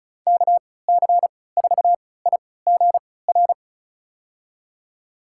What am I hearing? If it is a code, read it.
Morse code, heard as a keyed tone, about 36 wpm, spaced as for 20 wpm, 700 Hz. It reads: KC4IGR
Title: USA Call Signs - 3x Character Spacing - 35wpm